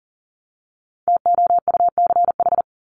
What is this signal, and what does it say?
Morse code, heard as a keyed tone, about 40 wpm, 700 Hz. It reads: TOUCH
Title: Top 1001-1250 Words 40wpm